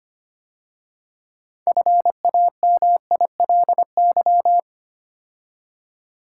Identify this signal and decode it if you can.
Morse code, heard as a keyed tone, about 30 wpm, 690 Hz. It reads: FAMILY
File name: Easy Words - Speed-Racing with Hints - 25wpm